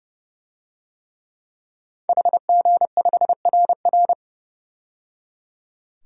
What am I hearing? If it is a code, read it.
Morse code, heard as a keyed tone, about 30 wpm, 700 Hz. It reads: HG5RR
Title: DXCC-100 Call Signs - 5 Characters - 30wpm